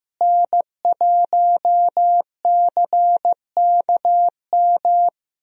Morse code, heard as a keyed tone, about 15 wpm, 700 Hz. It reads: N1CKM
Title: USA Call Signs - 5 Characters - Repeated 3x - 15wpm